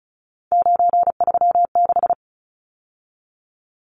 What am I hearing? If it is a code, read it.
Morse code, heard as a keyed tone, about 34 wpm, 700 Hz. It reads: 936
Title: Numbers - 3-Digit 35wpm